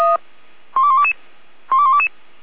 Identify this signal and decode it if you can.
Five-tone selective call (ZVEI), heard as quick runs of five tones, tones of about 70 ms. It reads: 12122 12122